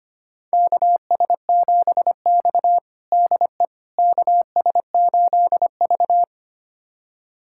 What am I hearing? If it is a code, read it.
Morse code, heard as a keyed tone, about 25 wpm, 700 Hz. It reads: KS7X DE KS8V
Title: USA Callsign DE Callsign - 4 Characters - 25wpm